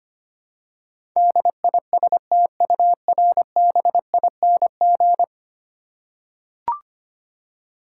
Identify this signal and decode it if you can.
Morse code, heard as a keyed tone, about 25 wpm, 700 Hz. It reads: DISTURBING
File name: 10 Letter Words - First 3 Letters Given - 25wpm